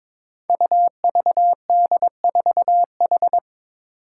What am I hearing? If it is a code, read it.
Morse code, heard as a keyed tone, about 22 wpm, 700 Hz. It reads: UVD4H
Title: Five Challenging Letter-Number Combinations 22wpm